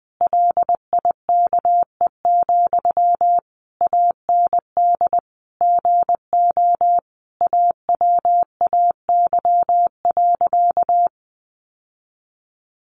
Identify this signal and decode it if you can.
Morse code, heard as a keyed tone, about 20 wpm, 700 Hz. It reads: LIKE, AND GO AWAY.
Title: Sentences from Top 1100 Words 20wpm